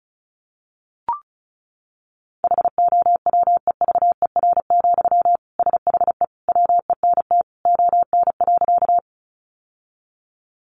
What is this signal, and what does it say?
Morse code, heard as a keyed tone, about 34 wpm, 700 Hz. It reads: HOWEVER, SHE WENT ON.
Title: Sentences from Top 600 Words 35wpm